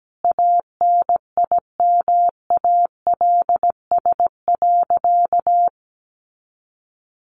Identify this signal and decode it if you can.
Morse code, heard as a keyed tone, about 17 wpm, 700 Hz. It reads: ANIMALS.
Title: A Sound Of Thunder 17wpm